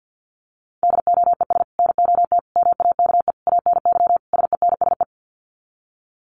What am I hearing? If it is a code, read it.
Morse code, heard as a keyed tone, about 50 wpm, 700 Hz. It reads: DOES NOT MAKE ANY SENSE
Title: Sets of 5 Words Encore2 50wpm